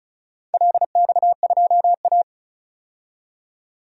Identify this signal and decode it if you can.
Morse code, heard as a keyed tone, about 36 wpm, 700 Hz. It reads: LX2A
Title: DXCC-100 Call Signs - 4 Characters - 35wpm